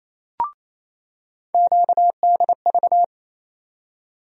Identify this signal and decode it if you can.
Morse code, heard as a keyed tone, about 27 wpm, 700 Hz. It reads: QDV
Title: USA Call Sign Postfixes - 3 Letters 28wpm